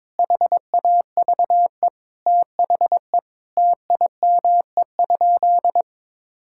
Morse code, heard as a keyed tone, about 22 wpm, 700 Hz. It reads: HAVE THE TIME?
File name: practice / 22wpm / Sentences with 2-Word Phrases - 22wpm